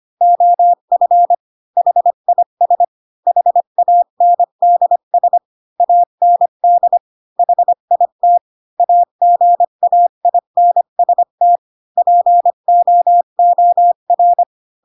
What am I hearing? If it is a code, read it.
Morse code, heard as a keyed tone, about 25 wpm, 700 Hz. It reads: OF HIS HANDS AND HIT AGAINST POOR